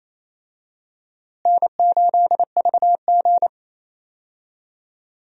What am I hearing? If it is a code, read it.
Morse code, heard as a keyed tone, about 28 wpm, 700 Hz. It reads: N8VG
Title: USA Call Signs - 4 Characters - Repeated 2x - 28wpm